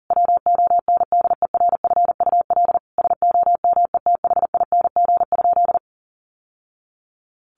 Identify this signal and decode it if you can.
Morse code, heard as a keyed tone, about 40 wpm, 700 Hz. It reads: WONDERFUL SOMETHING?
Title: Sentences from Top 1100 Words 40wpm